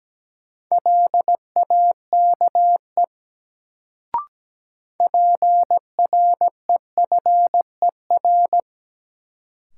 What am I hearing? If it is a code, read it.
Morse code, heard as a keyed tone, about 17 wpm, 700 Hz. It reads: LAKE PREFER
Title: Sentences from Top 1100 Words - New Word Drill Alt - 17wpm